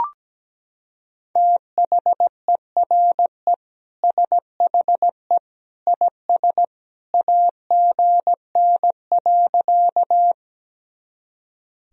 Morse code, heard as a keyed tone, about 17 wpm, 700 Hz. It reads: THERE SHE IS AGN.
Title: Sentences from Top 200 Words and Abbreviations 17wpm